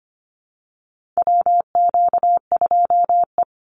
Morse code, heard as a keyed tone, about 25 wpm, 700 Hz. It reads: WQ2E
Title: Call Signs Repeated 3x - 25wpm